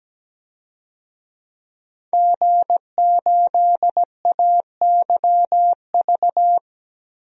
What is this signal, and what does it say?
Morse code, heard as a keyed tone, about 17 wpm, 700 Hz. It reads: G8AYV
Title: DXCC-100 Call Signs - 5 Characters - 17wpm